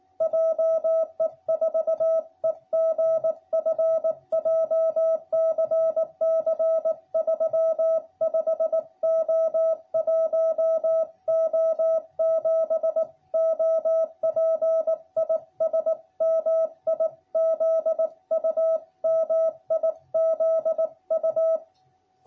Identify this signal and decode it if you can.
Morse code, heard as a keyed tone, about 18 wpm, 660 Hz. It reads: JE4EGF JCC 35O1O7 OP IS MIZU MIZU